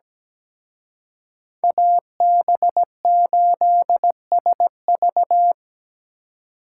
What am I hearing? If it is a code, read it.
Morse code, heard as a keyed tone, about 20 wpm, 700 Hz. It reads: AB8SV
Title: Speed Racing - USA 2x2 Call Signs 17wpm